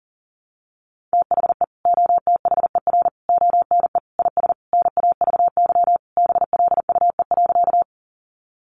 Morse code, heard as a keyed tone, about 40 wpm, 700 Hz. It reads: THE OTHER ONE IS NAVY BLUE.
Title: Additional Easy Phrases 40wpm